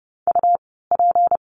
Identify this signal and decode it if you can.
Morse code, heard as a keyed tone, about 30 wpm, spaced as for 17 wpm, 700 Hz. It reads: UP
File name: course / 30:10wpm / 230 Lesson 37f - Letter x - Words 30wpm